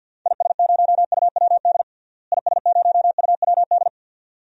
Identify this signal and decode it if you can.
Morse code, heard as a keyed tone, about 48 wpm, 700 Hz. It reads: IS0UWD IS0UWD
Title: DXCC-100 Call Signs - 6 Characters - Repeated 2x - 50wpm